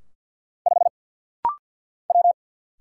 Morse code, heard as a keyed tone, about 50 wpm, 700 Hz. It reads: 5 W
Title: Single Letter-Number - Rapid-Fire 50wpm